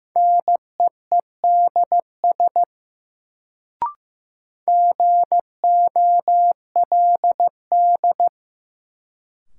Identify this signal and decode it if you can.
Morse code, heard as a keyed tone, about 15 wpm, 700 Hz. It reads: NEEDS GOLD
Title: Sentences from Top 1000 Words - Review Part 3 - New Word Drill - 15wpm